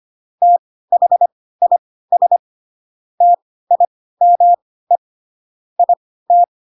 Morse code, heard as a keyed tone, about 25 wpm, spaced as for 16 wpm, 700 Hz. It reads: THIS TIME IT
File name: Pinocchio / Pinocchio ch1 10 25 ik1hge